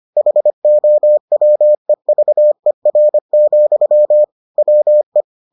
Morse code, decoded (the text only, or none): HOWEVER, WE